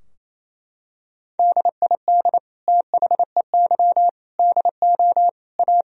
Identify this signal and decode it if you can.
Morse code, heard as a keyed tone, about 28 wpm, 700 Hz. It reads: DID THEY DO A
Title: Sentences from Top 1000 Words - Review Part 3 - 28wpm